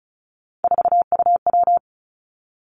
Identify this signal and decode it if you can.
Morse code, heard as a keyed tone, about 36 wpm, 700 Hz. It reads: 4UW
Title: Three Challenging Letter-Number Combinations 35wpm